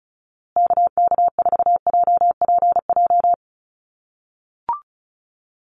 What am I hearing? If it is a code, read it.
Morse code, heard as a keyed tone, about 34 wpm, 700 Hz. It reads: KK4JPJ
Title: USA Call Signs Repeated 2x 35wpm